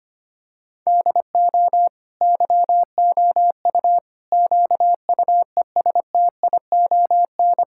Morse code, heard as a keyed tone, about 25 wpm, 700 Hz. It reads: DO YOU QUESTION
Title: Sentences from Top 1000 Words - Review Part 5 - 25wpm